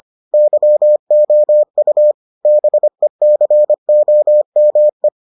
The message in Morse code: YOU BECOME

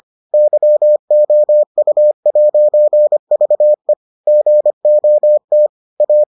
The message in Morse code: YOU'VE GOT A

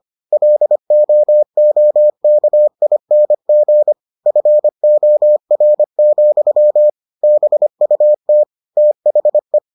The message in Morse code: LOOKING FOR, BUT THE

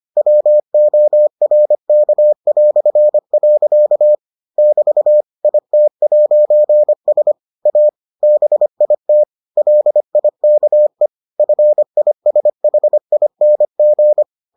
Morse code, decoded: WORK". <BT> IT'S A BIT LIKE FISHING

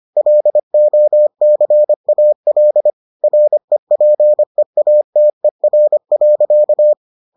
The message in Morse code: LOCAL REPEATER.

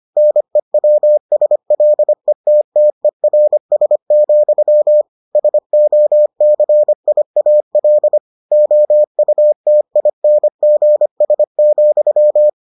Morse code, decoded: NEWSLETTERS, SOCIAL OUTINGS,

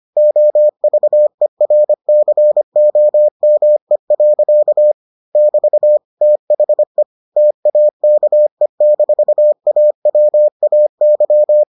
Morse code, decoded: OVERCOME. <BT> THE TAKE-AWAY

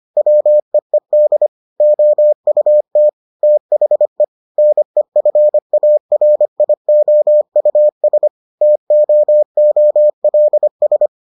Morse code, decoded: WEED OUT THE NEFARIOUS TOOLS

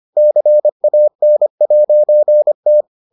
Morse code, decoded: CAN'T

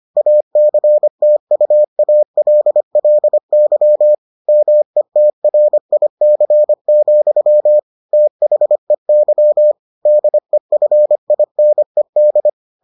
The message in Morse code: ACTUALLY METRIC, THEY DEFINED

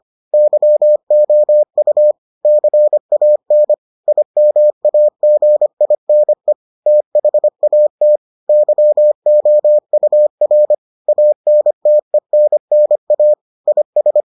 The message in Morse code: YOU CAN IMAGINE THAT YOUR ANTENNA IS

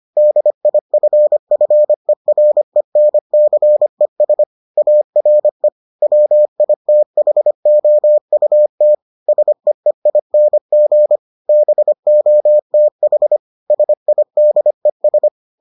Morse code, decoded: DIFFERENCES ARE WITHOUT SEEING BOTH SIDES

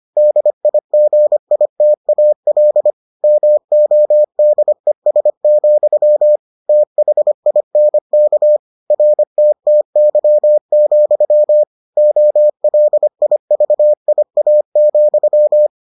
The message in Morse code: DIGITAL MODES, THINK RTTY, OLIVIA,

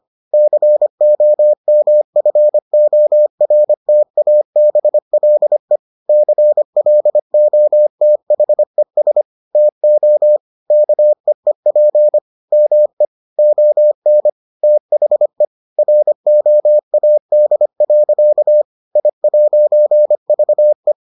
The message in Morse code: COMFORTABLE CLOTHES TO KEEP ME ON THE ROAD. I'VE

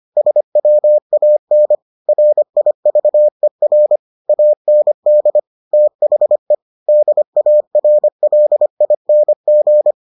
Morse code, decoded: SWAN RIVER AND THE DARLING

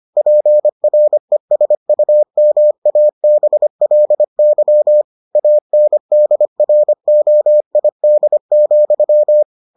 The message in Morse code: PRESUMABLY ANDROID,

